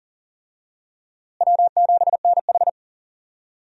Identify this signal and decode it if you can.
Morse code, heard as a keyed tone, about 40 wpm, 700 Hz. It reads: W7NH
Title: USA Call Signs - 4 Characters - Repeated 3x - 40wpm